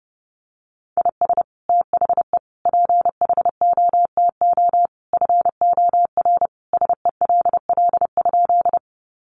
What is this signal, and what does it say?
Morse code, heard as a keyed tone, about 30 wpm, 700 Hz. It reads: IS THE PHOTO FOR SELL?
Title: Sentences from Top 1200 Words 30wpm